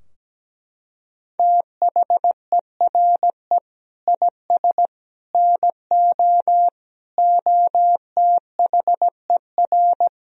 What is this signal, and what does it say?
Morse code, heard as a keyed tone, about 17 wpm, 700 Hz. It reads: THERE IS NO OTHER
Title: Sentences from Top 1000 Words - Review Part 2 - 17wpm